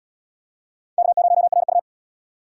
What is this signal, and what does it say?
Morse code, heard as a keyed tone, about 67 wpm, 700 Hz. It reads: K0RU